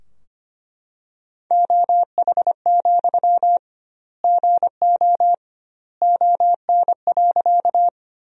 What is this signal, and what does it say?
Morse code, heard as a keyed tone, about 25 wpm, 700 Hz. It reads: OH, GO ON.
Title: 2x Word-Spacing - Sentences from Top 500 Words - 25wpm